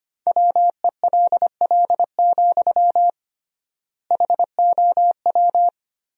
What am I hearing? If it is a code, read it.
Morse code, heard as a keyed tone, about 25 wpm, 700 Hz. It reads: WELL, HOW
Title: 3x Word-Spacing - Sentences from Top 200 Words - 25wpm